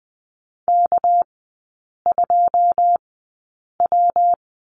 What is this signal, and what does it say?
Morse code, heard as a keyed tone, about 20 wpm, spaced as for 8 wpm, 700 Hz. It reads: K2W
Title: 198 Lesson 32e - Letter k - Three Characters 20wpm